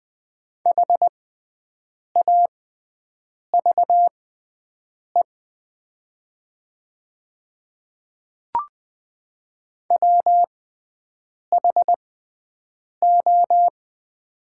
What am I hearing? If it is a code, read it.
Morse code, heard as a keyed tone, about 20 wpm, spaced as for 7 wpm, 700 Hz. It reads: HAVE WHO
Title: 6x Character Spacing - Top 100 Words - 20wpm